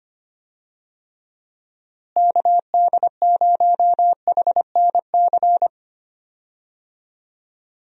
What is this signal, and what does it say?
Morse code, heard as a keyed tone, about 25 wpm, 700 Hz. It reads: KD0HNC
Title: USA Call Signs - 6 Characters - Repeated 2x - 25wpm